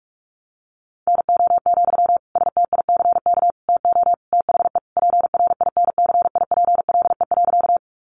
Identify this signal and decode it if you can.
Morse code, heard as a keyed tone, about 44 wpm, 700 Hz. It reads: NO, STICK TO THE PRINCIPLE.
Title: Sentences from Top 1400 Words 45wpm